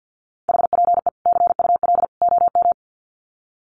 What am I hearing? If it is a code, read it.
Morse code, heard as a keyed tone, about 50 wpm, 700 Hz. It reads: HPE CUL OM
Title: Common QSO Phrases 50wpm